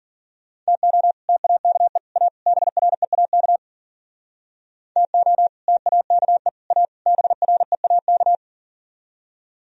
Morse code, heard as a keyed tone, about 52 wpm, 700 Hz. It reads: TO TAKE A BREAK TO TAKE A BREAK